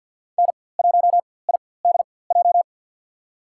Morse code, heard as a keyed tone, about 50 wpm, spaced as for 23 wpm, 700 Hz. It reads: N1IDJ
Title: USA Call Signs - 5 Characters - 4x Character Spacing - 50wpm